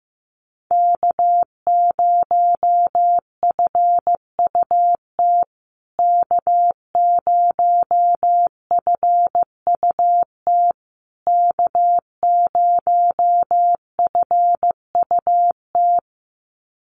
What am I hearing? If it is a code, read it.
Morse code, heard as a keyed tone, about 15 wpm, 700 Hz. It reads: K0FUT K0FUT K0FUT
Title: USA Call Signs Repeated 3x 15wpm